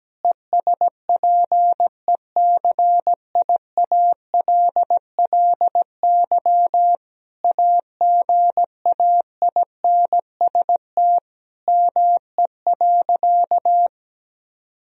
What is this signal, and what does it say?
Morse code, heard as a keyed tone, about 17 wpm, 700 Hz. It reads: ESPECIALLY AGAINST ME.
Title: Sentences from Top 1000 Words - Review Part 4 - 17wpm